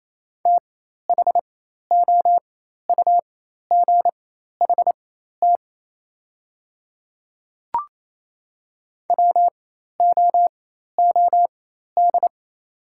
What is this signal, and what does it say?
Morse code, heard as a keyed tone, about 28 wpm, spaced as for 13 wpm, 700 Hz. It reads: THOUGHT WOOD
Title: 4x Character Spacing - Top 300 Words - 28wpm